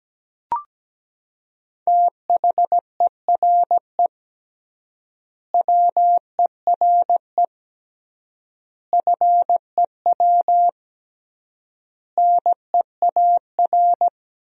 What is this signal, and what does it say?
Morse code, heard as a keyed tone, about 17 wpm, 700 Hz. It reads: THERE WERE FEW NEAR
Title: 3x Word-Spacing - Sentences from Top 300 Words - 17wpm